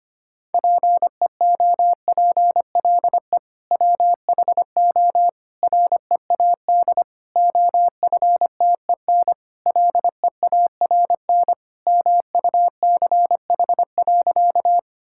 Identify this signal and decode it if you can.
Morse code, heard as a keyed tone, about 25 wpm, 700 Hz. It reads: PEOPLE WHO READ OFTEN LEARN MUCH.
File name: practice / 25wpm / Sentences with 2-Word Phrases - 25wpm